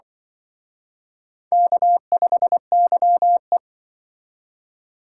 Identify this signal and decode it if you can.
Morse code, heard as a keyed tone, about 24 wpm, 700 Hz. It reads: K5YE
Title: Speed Racing - USA Call Signs 20wpm